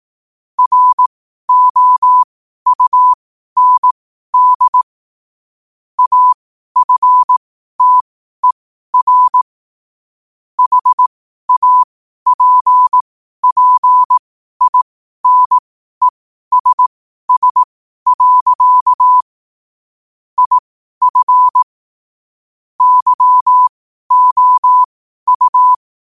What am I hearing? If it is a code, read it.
Morse code, heard as a keyed tone, about 18 wpm, spaced as for 12 wpm, 1000 Hz. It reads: ROUND AFTER HAPPINESS. IF YOU
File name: qotdfiles / 12wpm